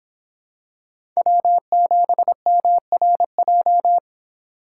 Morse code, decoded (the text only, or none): W7MRJ